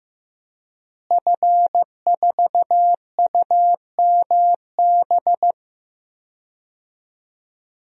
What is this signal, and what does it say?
Morse code, heard as a keyed tone, about 15 wpm, 700 Hz. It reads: F4UMB
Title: DXCC-100 Call Signs - 5 Characters - Repeated 3x - 15wpm